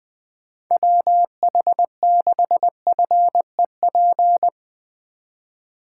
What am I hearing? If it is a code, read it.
Morse code, heard as a keyed tone, about 20 wpm, 700 Hz. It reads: WH6FEP
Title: USA Call Signs - 6 Characters - 2x Character Spacing - 20wpm